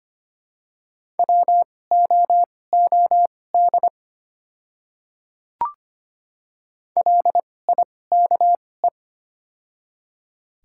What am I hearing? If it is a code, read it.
Morse code, heard as a keyed tone, about 25 wpm, spaced as for 18 wpm, 690 Hz. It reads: WOOD LIKE
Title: 2x Character Spacing - Top 300 Words - 25wpm